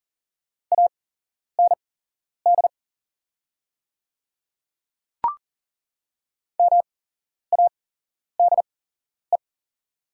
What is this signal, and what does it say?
Morse code, heard as a keyed tone, about 40 wpm, spaced as for 11 wpm, 700 Hz. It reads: AND MADE
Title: 8x Character Spacing - Top 200 Words - 40wpm